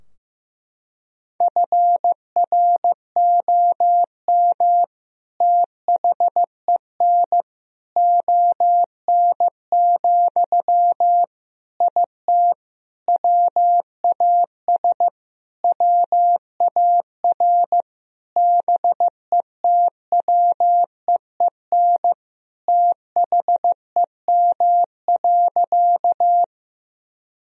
Morse code, heard as a keyed tone, about 15 wpm, 700 Hz. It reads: FROM THEN ON, IT WAS WAR BETWEEN THEM.
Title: Sentences from Top 1000 Words - Review Part 2 - 15wpm